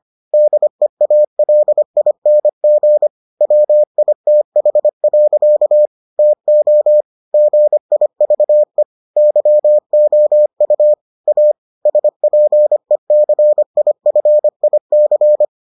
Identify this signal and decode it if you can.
Morse code, heard as a keyed tone, about 25 wpm, 600 Hz. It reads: DEALING WITH. TO GIVE YOU A SPECIFIC